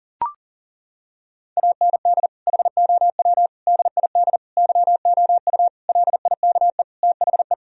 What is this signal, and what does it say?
Morse code, heard as a keyed tone, about 37 wpm, 700 Hz. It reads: AND HOW DID YOU LIKE THE
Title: Sentences from Top 1000 Words - Review Part 4 - 40wpm